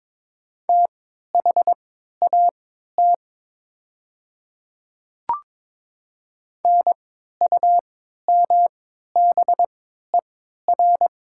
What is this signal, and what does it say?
Morse code, heard as a keyed tone, about 22 wpm, spaced as for 12 wpm, 700 Hz. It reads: THAT NUMBER
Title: 3x Character Spacing - Top 200 Words - 22wpm